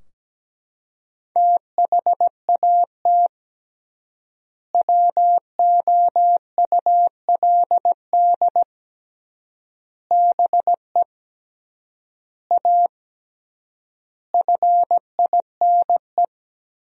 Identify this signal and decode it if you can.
Morse code, heard as a keyed tone, about 17 wpm, 700 Hz. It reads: THAT WOULD BE A FINE
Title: 3x Word-Spacing - Sentences from Top 500 Words - 17wpm